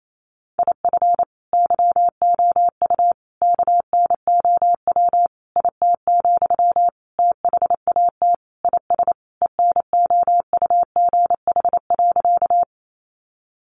Morse code, heard as a keyed tone, about 28 wpm, 700 Hz. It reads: IF YOU KNOW IT, THAT IS ENOUGH.
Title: Sentences from Top 400 Words 28wpm